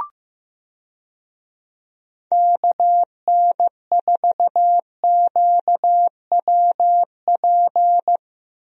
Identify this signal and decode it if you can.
Morse code, heard as a keyed tone, about 15 wpm, 700 Hz. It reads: KN4QWP
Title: USA Call Signs - 6 Characters - Repeated 3x - 15wpm